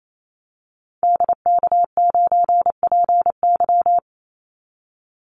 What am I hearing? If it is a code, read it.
Morse code, heard as a keyed tone, about 28 wpm, 700 Hz. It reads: DK9PY
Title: Call Signs Repeated 2x - 28wpm